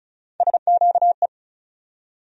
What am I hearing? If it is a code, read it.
Morse code, heard as a keyed tone, about 36 wpm, 700 Hz. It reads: SQE
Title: USA Call Sign Postfixes - 3 Letters 35wpm